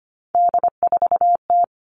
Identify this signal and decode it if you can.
Morse code, heard as a keyed tone, about 25 wpm, 700 Hz. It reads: D4T
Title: All Cut Numbers - 3-Digit 25wpm